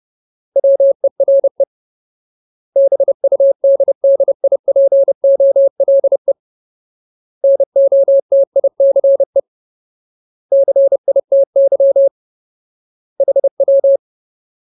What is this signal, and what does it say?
Morse code, heard as a keyed tone, about 30 wpm, 550 Hz. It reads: WERE BUDDIPOLE NOTICE CITY HW
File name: Ultimate Word List / 30WPM Ultimate